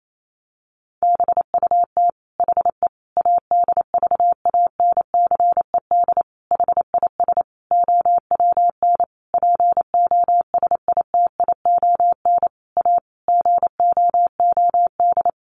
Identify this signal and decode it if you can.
Morse code, heard as a keyed tone, about 28 wpm, 700 Hz. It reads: BUT HE ADVANCED HIS OWN POSITION A GOOD
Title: Sentences from Top 1200 Words 28wpm